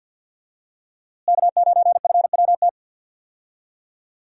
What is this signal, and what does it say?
Morse code, heard as a keyed tone, about 51 wpm, 700 Hz. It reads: K9FWT
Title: USA Call Signs - 5 Characters - Repeated 3x - 50wpm